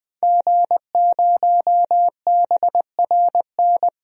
Morse code, heard as a keyed tone, about 20 wpm, 700 Hz. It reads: G0BRN
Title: DXCC-100 Call Signs - 5 Characters - 20wpm